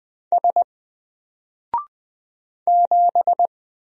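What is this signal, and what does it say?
Morse code, heard as a keyed tone, about 20 wpm, 700 Hz. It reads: S 7
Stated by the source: Single letter and number Morse drill at 20 wpm, from N4LCD.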